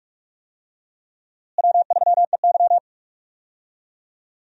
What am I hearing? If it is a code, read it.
Morse code, heard as a keyed tone, about 46 wpm, 700 Hz. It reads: W3EY
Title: USA Call Signs - 4 Characters - Repeated 2x - 45wpm